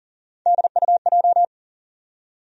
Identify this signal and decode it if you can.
Morse code, heard as a keyed tone, about 40 wpm, 700 Hz. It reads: DUJ
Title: USA Call Sign Postfixes - 3 Letters 40wpm